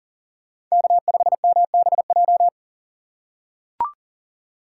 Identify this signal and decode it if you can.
Morse code, heard as a keyed tone, about 40 wpm, 700 Hz. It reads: K5MBJ
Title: USA Call Signs - 5 Characters - 2x Character Spacing - 40wpm